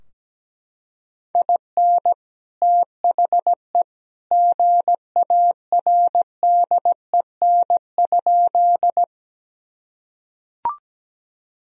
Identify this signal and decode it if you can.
Morse code, heard as a keyed tone, about 17 wpm, 700 Hz. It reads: IN THE GARDEN?
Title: Sentences from Top 1000 Words - Review Part 4 - 17wpm